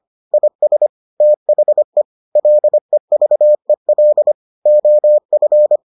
Morse code, heard as a keyed tone, about 25 wpm, 600 Hz. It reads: IS THE LEVEL OF